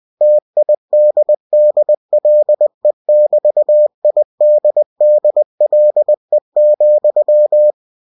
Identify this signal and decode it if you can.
Morse code, heard as a keyed tone, about 20 wpm, 600 Hz. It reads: TIDDLE<BT>IDDLE,
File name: cw-books / winnie-the-pooh / winnie-20wpm-page-0013